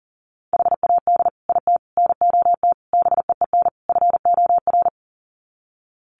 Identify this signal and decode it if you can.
Morse code, heard as a keyed tone, about 40 wpm, 700 Hz. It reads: HAD IT NOT BEEN FOR
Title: Sets of 5 Words Encore2 40wpm